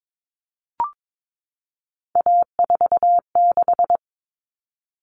Morse code, heard as a keyed tone, about 22 wpm, 700 Hz. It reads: A46
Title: Common Cut Numbers - 3-Digit 22wpm